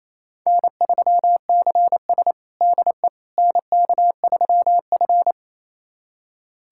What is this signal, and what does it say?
Morse code, heard as a keyed tone, about 28 wpm, 700 Hz. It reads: N3CS DE NK3F
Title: USA Callsign DE Callsign - 4 Characters - 28wpm